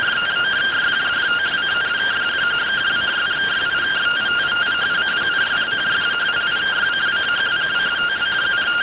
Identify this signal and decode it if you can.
Two-tone teleprinter signal, about 170 Hz shift.